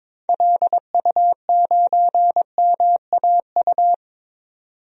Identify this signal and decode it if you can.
Morse code, heard as a keyed tone, about 21 wpm, 700 Hz. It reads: LU9MAU
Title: DXCC-100 Call Signs - 6 Characters - 22wpm